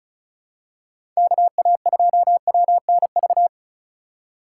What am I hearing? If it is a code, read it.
Morse code, heard as a keyed tone, about 36 wpm, 700 Hz. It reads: KA2WNV
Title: USA Call Signs - 6 Characters - 2x Character Spacing - 35wpm